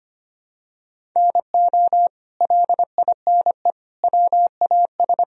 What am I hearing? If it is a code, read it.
Morse code, heard as a keyed tone, about 25 wpm, 700 Hz. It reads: NO LINE WAS
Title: Sentences from Top 1000 Words - Review Part 1 - 25wpm